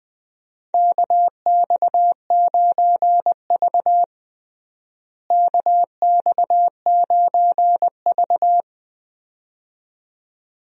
Morse code, heard as a keyed tone, about 20 wpm, 700 Hz. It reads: KX9V KX9V
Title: USA Call Signs - 4 Characters - Repeated 2x - 20wpm